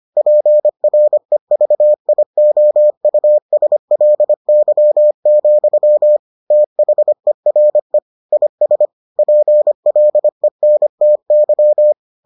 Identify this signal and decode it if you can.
Morse code, heard as a keyed tone, about 25 wpm, 600 Hz. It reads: PREVIOUSLY, THERE IS PLENTY